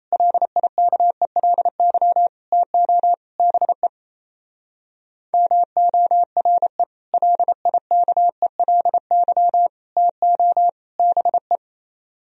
Morse code, heard as a keyed tone, about 34 wpm, 700 Hz. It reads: LIKELY TO BE MORE LIKELY TO BE
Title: Speed-Racing - Sets of 4 Words 28wpm